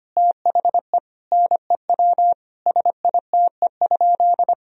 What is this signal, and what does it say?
Morse code, heard as a keyed tone, about 25 wpm, 700 Hz. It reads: THE NEW SITE?
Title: Sentences from Top 1000 Words - Review Part 4 - 25wpm